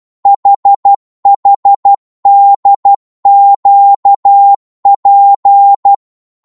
Morse code, heard as a keyed tone, about 12 wpm, 800 Hz. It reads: HHDQP
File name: Morse / ltr-12wpm